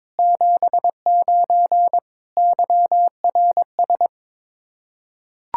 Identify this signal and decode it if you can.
Morse code, heard as a keyed tone, about 22 wpm, 700 Hz. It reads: 79 YRS
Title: Practice QSO - Exchange 3 - More Info - 22wpm